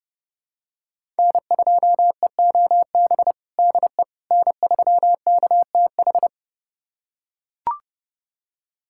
Morse code, heard as a keyed tone, about 30 wpm, 700 Hz. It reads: N2EOB DE N3KTH
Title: USA Callsign DE Callsign - 5 Characters - 30wpm